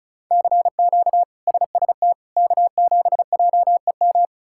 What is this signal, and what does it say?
Morse code, heard as a keyed tone, about 36 wpm, 700 Hz. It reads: CQ SST K7JEM
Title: K1USN SST QSO Practice - 35wpm